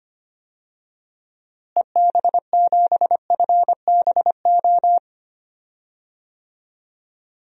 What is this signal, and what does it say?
Morse code, heard as a keyed tone, about 25 wpm, 700 Hz. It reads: EB7FBO
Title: DXCC-100 Call Signs - 6 Characters - Repeated 2x - 25wpm